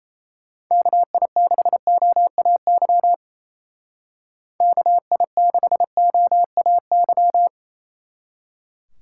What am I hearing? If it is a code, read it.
Morse code, heard as a keyed tone, about 35 wpm, 700 Hz. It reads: KI6OAY KI6OAY
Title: Speed Racing - USA 2x3 Call Signs 28wpm